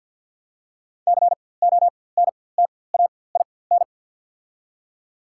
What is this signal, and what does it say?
Morse code, heard as a keyed tone, about 51 wpm, spaced as for 23 wpm, 700 Hz. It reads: CONTAIN
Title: 4x Character Spacing - Top 500 Words - 50wpm